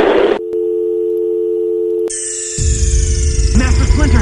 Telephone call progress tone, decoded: dial tone